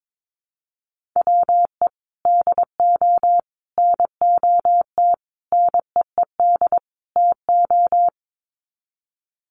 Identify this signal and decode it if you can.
Morse code, heard as a keyed tone, about 22 wpm, 700 Hz. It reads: WE DO NOT NEED TO
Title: Sets of 5 Words Encore 22wpm